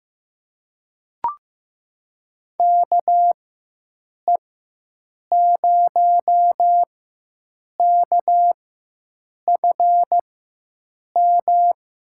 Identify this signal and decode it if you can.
Morse code, heard as a keyed tone, about 15 wpm, spaced as for 7 wpm, 700 Hz. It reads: KE0KFM
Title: USA Call Signs - 4x Character Spacing - 15wpm